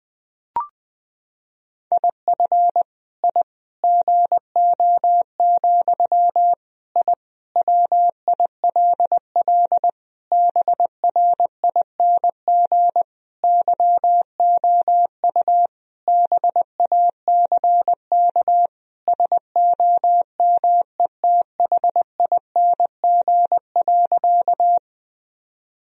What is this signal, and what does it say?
Morse code, heard as a keyed tone, about 20 wpm, 700 Hz. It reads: IF I GO, I WILL BRING YOU BACK SOMETHING.
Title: Sentences from Top 1000 Words - Review Part 3 - 20wpm